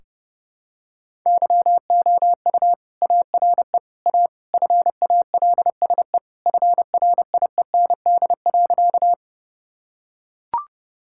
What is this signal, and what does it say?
Morse code, heard as a keyed tone, about 32 wpm, 700 Hz. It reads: YOU ARE A FALSE FRIEND.
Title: Sentences from Top 1600 Words 30wpm